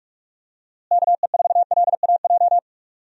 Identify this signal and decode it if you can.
Morse code, heard as a keyed tone, about 46 wpm, 700 Hz. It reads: KE4LAJ
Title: USA Call Signs - 6 Characters - Repeated 3x - 45wpm